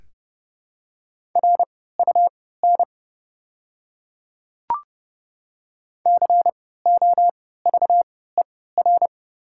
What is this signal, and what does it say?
Morse code, heard as a keyed tone, about 30 wpm, spaced as for 17 wpm, 700 Hz. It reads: RUN COVER
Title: 3x Character Spacing - Top 300 Words - 30wpm